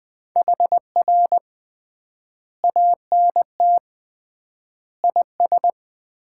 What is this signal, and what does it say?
Morse code, heard as a keyed tone, about 20 wpm, 700 Hz. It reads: HR ANT IS
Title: Common QSO Phrases - 3x Word-Spacing - 20wpm